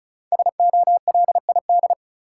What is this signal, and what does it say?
Morse code, heard as a keyed tone, about 36 wpm, 700 Hz. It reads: SOLID
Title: Sentences from Top 1500 Words - New Word Drill Alt - 35wpm